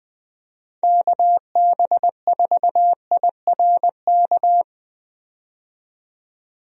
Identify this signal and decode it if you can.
Morse code, heard as a keyed tone, about 20 wpm, 700 Hz. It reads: KB4IRK
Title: USA Call Signs - 6 Characters - 20wpm